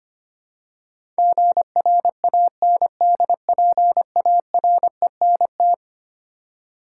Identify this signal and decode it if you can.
Morse code, heard as a keyed tone, about 25 wpm, 700 Hz. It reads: GRANDPARENT
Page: morse-code-ninja-practice-25wpm